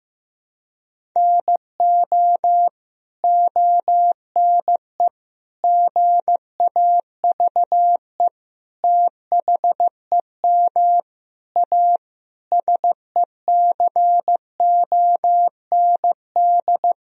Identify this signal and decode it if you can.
Morse code, heard as a keyed tone, about 15 wpm, 700 Hz. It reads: NO ONE GAVE THEM A SECOND
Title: Sentences from Top 1000 Words - Review Part 1 - 15wpm